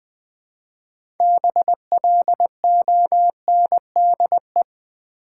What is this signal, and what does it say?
Morse code, heard as a keyed tone, about 20 wpm, 700 Hz. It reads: BLONDE
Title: 6 Letter Words - First 3 Letters Given - 20wpm